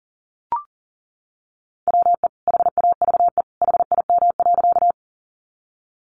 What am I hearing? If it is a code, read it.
Morse code, heard as a keyed tone, about 40 wpm, 700 Hz. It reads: WE HAVE HIM.
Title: Sentences from Top 100 Words 40wpm